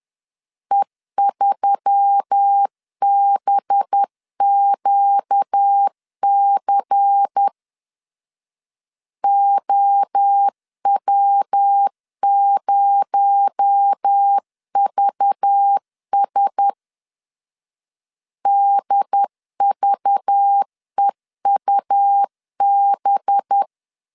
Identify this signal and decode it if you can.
Morse code, heard as a keyed tone, about 10 wpm, 800 Hz. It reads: E3BQC OW0VS DVEUB
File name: LEZIONI / 6 PKX0 / 6 PKX0 10wpm 2707